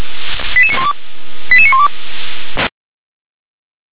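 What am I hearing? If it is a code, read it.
Five-tone selective call (ZVEI), heard as quick runs of five tones, tones of about 70 ms. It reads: 88912 88912
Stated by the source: ZVEI five-tone selective call received by SDR.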